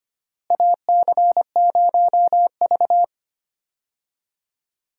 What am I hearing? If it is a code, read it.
Morse code, heard as a keyed tone, about 25 wpm, 700 Hz. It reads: AC0V
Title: USA Call Signs - 4 Characters - Repeated 3x - 25wpm